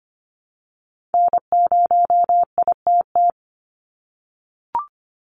Morse code, heard as a keyed tone, about 25 wpm, 700 Hz. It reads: N0ITT
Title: USA Call Signs 25wpm